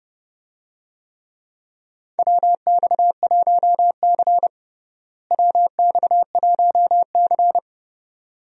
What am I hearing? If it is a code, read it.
Morse code, heard as a keyed tone, about 31 wpm, 700 Hz. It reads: WX1C WX1C
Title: USA Call Signs - 4 Characters - Repeated 2x - 30wpm